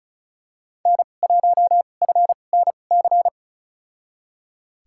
Morse code, heard as a keyed tone, about 36 wpm, spaced as for 25 wpm, 700 Hz. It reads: N1FNC
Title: USA Call Signs - 5 Characters - 2x Character Spacing - 35wpm